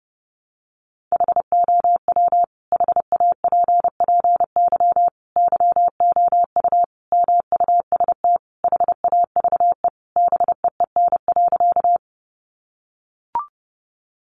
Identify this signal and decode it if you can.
Morse code, heard as a keyed tone, about 28 wpm, 700 Hz. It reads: HOW HAPPY YOU MUST HAVE BEEN.
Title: Sentences from Top 1000 Words 30wpm